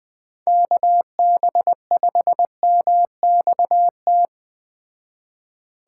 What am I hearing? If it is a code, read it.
Morse code, heard as a keyed tone, about 20 wpm, 700 Hz. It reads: KB5MXT
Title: Speed Racing - USA Call Signs 17wpm